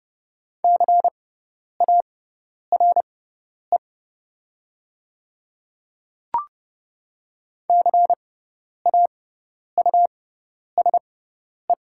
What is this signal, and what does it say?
Morse code, heard as a keyed tone, about 30 wpm, spaced as for 10 wpm, 700 Hz. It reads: CARE CAUSE
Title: 6x Character Spacing - Top 300 Words - 30wpm